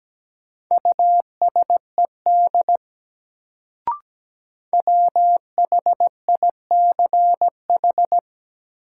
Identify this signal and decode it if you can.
Morse code, heard as a keyed tone, about 17 wpm, 700 Hz. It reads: USED WHICH
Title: Sets of 2 Words - New Word Drill - 17wpm